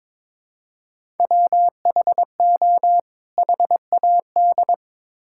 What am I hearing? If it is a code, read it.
Morse code, heard as a keyed tone, about 22 wpm, 700 Hz. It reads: WHO HAD
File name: Build A Phrase - 2-Word Phrases - 22wpm